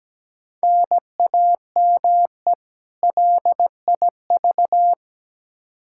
Morse code, heard as a keyed tone, about 17 wpm, 700 Hz. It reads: NAME LIV
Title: Practice QSO - Exchange 1 - Beginning - 17wpm